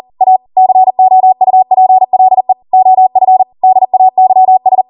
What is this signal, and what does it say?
Morse code, heard as a keyed tone, about 40 wpm, 750 Hz. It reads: A COUPLE OF DAYS